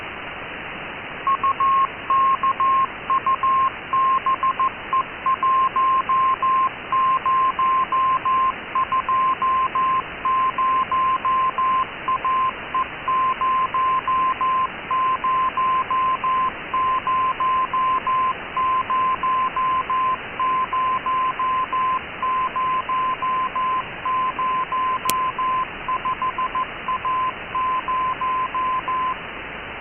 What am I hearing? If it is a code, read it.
Morse code, heard as a keyed tone, about 14 wpm, 1040 Hz, with noise. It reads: UKUBE1020AE00000005A0
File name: ukube cw 12dec2014 2130z